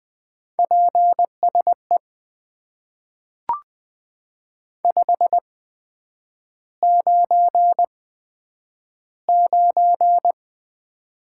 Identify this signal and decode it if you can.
Morse code, heard as a keyed tone, about 20 wpm, spaced as for 9 wpm, 700 Hz. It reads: PSE 599
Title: Top 90 QSO Elements - 8x Character Spacing - 20wpm